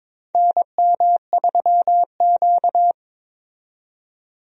Morse code, heard as a keyed tone, about 22 wpm, 700 Hz. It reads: NM3Q
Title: USA Call Signs - 4 Characters - 22wpm